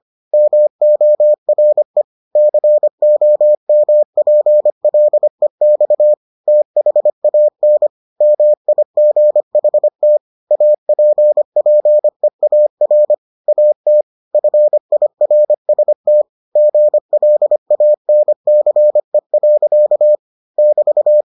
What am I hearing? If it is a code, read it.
Morse code, heard as a keyed tone, about 25 wpm, 600 Hz. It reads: MORE COMPLEX THAN MIGHT APPEAR AT FIRST GLANCE. <BT>